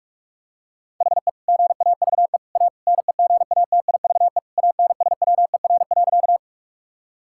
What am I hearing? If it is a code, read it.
Morse code, heard as a keyed tone, about 42 wpm, 700 Hz. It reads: HE GAVE A NEGATIVE ANSWER.
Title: Sentences from Top 1000 Words - Review Part 6 - 45wpm